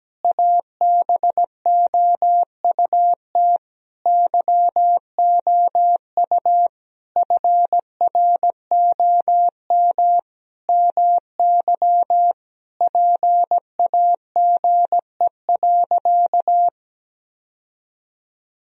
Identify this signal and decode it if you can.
Morse code, heard as a keyed tone, about 17 wpm, 700 Hz. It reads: ABOUT YOU FROM MY PAGE.
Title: Sentences from Top 1000 Words - Review Part 1 - 17wpm